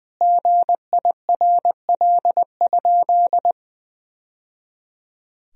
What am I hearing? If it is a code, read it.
Morse code, heard as a keyed tone, about 21 wpm, 700 Hz. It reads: GIRL?
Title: Sentences from Top 1000 Words - Review Part 4 - 20wpm